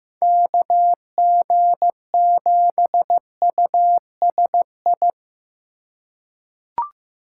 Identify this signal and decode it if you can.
Morse code, heard as a keyed tone, about 15 wpm, 700 Hz. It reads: KG7USI
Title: USA Call Signs - 6 Characters - 3x Character Spacing - 15wpm